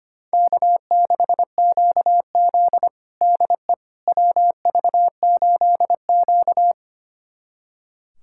Morse code, heard as a keyed tone, about 25 wpm, 700 Hz. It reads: K6QZ DE WV8Q
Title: USA Callsign DE Callsign - 4 Characters - 25wpm